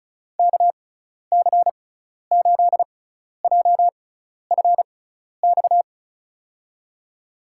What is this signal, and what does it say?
Morse code, heard as a keyed tone, about 36 wpm, spaced as for 12 wpm, 700 Hz. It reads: KC8JFX
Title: USA Call Signs - 6 Characters - 6x Character Spacing - 35wpm